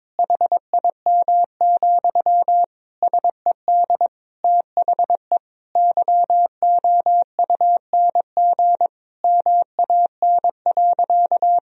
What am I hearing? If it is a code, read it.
Morse code, heard as a keyed tone, about 22 wpm, 700 Hz. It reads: HIM, SED THE YOUNG MAN.
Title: Sentences from Top 400 Words and Abbreviations 22wpm